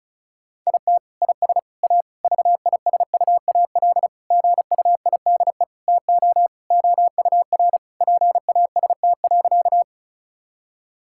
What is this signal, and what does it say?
Morse code, heard as a keyed tone, about 36 wpm, 700 Hz. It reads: IT IS A VISUAL GUIDE TO OUR PAST.
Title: Sentences from Top 1000 Words - Review Part 6 - 35wpm